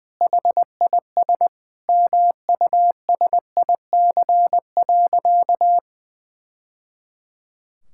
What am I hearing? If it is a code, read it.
Morse code, heard as a keyed tone, about 20 wpm, 700 Hz. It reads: HIS MUSIC.
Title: Sentences from Top 1000 Words - Review Part 6 - 20wpm